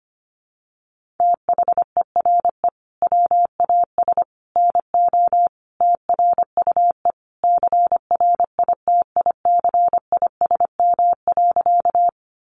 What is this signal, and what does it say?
Morse code, heard as a keyed tone, about 25 wpm, 700 Hz. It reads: THERE WAS NO TRUE CRITICISM.
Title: Sentences from Top 1300 Words 25wpm